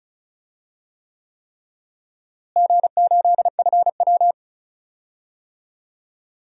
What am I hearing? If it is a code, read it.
Morse code, heard as a keyed tone, about 36 wpm, 700 Hz. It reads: G8FW
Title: DXCC-100 Call Signs - 4 Characters - Repeated 3x - 35wpm